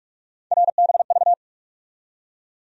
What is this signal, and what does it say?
Morse code, heard as a keyed tone, about 46 wpm, 700 Hz. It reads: RBV
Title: USA Call Sign Postfixes - 3 Letters 45wpm